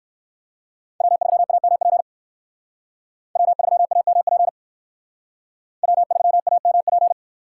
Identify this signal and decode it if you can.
Morse code, heard as a keyed tone, about 56 wpm, 700 Hz. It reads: W3AMP W3AMP W3AMP
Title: Speed Racing - USA 1x3 Call Signs 45wpm